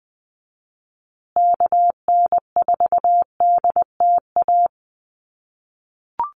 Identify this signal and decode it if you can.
Morse code, heard as a keyed tone, about 20 wpm, 700 Hz. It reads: KN4DTA
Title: USA Call Signs Repeated 2x 20wpm